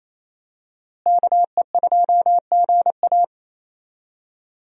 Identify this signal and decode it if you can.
Morse code, heard as a keyed tone, about 27 wpm, 700 Hz. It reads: KE2GA
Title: USA Call Signs - 5 Characters - Repeated 3x - 28wpm